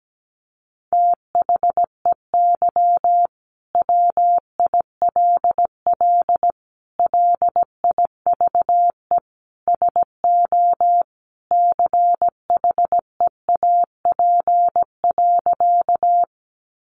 Morse code, heard as a keyed tone, about 17 wpm, 700 Hz. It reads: THEY WILL LIVE SO CHEAP.
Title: Sentences from Top 1200 Words 17wpm